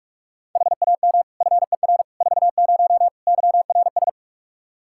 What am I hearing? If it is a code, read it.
Morse code, heard as a keyed tone, about 44 wpm, 700 Hz. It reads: HAM FER 40 YRS
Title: Practice QSO - Exchange 3 - More Info - 45wpm